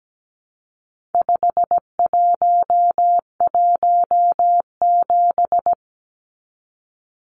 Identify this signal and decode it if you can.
Morse code, heard as a keyed tone, about 17 wpm, 700 Hz. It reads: 5117
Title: Numbers - 4-Digit 17wpm